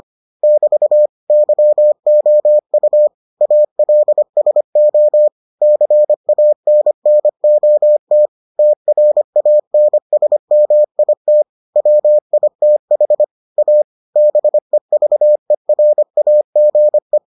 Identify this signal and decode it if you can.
Morse code, heard as a keyed tone, about 25 wpm, 600 Hz. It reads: <BT> YOU ALSO CANNOT TRANSMIT WITH A BEVERAGE